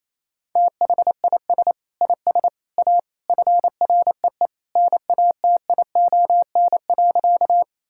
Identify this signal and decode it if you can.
Morse code, heard as a keyed tone, about 27 wpm, 700 Hz. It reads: THIS IS A FREE NATION.
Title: Sentences from Top 1000 Words - Review Part 6 - 28wpm